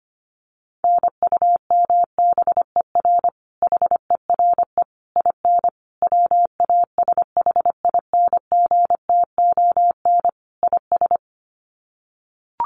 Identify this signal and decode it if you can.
Morse code, heard as a keyed tone, about 25 wpm, 700 Hz. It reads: NUMBER HERE IN WASHINGTON IS
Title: Sets of 5 Words Encore2 25wpm